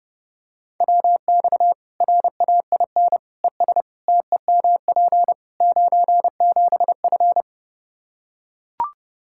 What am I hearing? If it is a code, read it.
Morse code, heard as a keyed tone, about 30 wpm, 700 Hz. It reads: WX RAIN ES TEMP 97F
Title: Practice QSO - Exchange 2 - Info - 30wpm